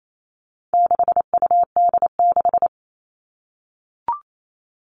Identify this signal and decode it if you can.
Morse code, heard as a keyed tone, about 28 wpm, 700 Hz. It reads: 6UD6
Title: All Cut Numbers - 4-Digit 28wpm